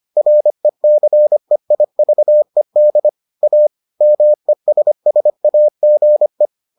Morse code, heard as a keyed tone, about 25 wpm, 600 Hz. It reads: RECEIVED A MESSAGE